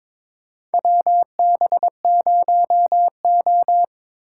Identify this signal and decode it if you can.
Morse code, heard as a keyed tone, about 22 wpm, 700 Hz. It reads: WB0O
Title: USA Call Signs - 8x Character Spacing - 22wpm